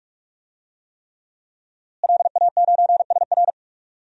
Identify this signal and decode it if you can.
Morse code, heard as a keyed tone, about 44 wpm, 700 Hz. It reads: LA9SR
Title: DXCC-100 Call Signs - 5 Characters - Repeated 3x - 45wpm